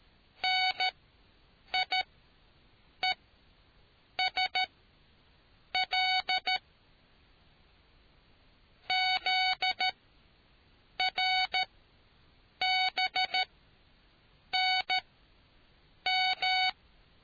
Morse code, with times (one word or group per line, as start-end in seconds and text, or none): 0.4-6.6 NIESL
8.9-16.7 ZRBNM